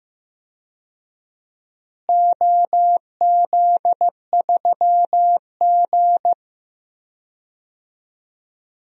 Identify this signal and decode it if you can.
Morse code, heard as a keyed tone, about 15 wpm, 700 Hz. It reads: OZ3G
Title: DXCC-100 Call Signs - 4 Characters - Repeated 2x - 15wpm